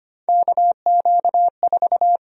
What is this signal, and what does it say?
Morse code, heard as a keyed tone, about 25 wpm, 700 Hz. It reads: KQ4
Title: USA Call Sign Prefixes - 25wpm